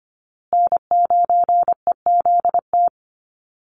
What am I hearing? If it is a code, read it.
Morse code, heard as a keyed tone, about 25 wpm, 700 Hz. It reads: N9EZT
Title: USA Call Signs Repeated 3x 25wpm